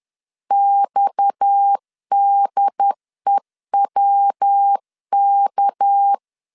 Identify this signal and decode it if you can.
Morse code, heard as a keyed tone, about 10 wpm, 800 Hz. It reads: XDEWK
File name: LEZIONI / 7 GF9Z / 7 GF 10wpm 2514